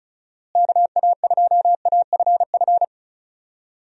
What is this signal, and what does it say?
Morse code, heard as a keyed tone, about 36 wpm, 700 Hz. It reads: KA2AFF
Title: USA Call Signs - 6 Characters - Repeated 3x - 35wpm